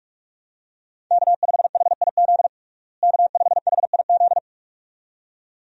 Morse code, heard as a keyed tone, about 41 wpm, 700 Hz. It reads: K5HIZ K5HIZ
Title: USA Call Signs - 5 Characters - Repeated 2x - 45wpm